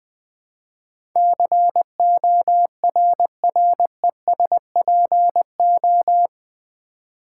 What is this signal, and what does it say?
Morse code, heard as a keyed tone, about 20 wpm, 700 Hz. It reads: CORRESPO
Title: Build A Word - 4 Letters At A Time - 10-Letter Words - 20wpm